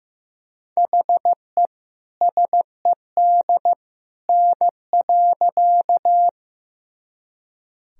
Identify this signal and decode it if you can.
Morse code, heard as a keyed tone, about 15 wpm, 700 Hz. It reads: HE SED N.
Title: Sentences from Top 100 Words and Abbreviations 15wpm